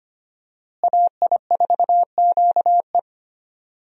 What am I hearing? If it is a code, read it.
Morse code, heard as a keyed tone, about 25 wpm, 700 Hz. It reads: AI4QE
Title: USA Call Signs - 5 Characters - Repeated 3x - 25wpm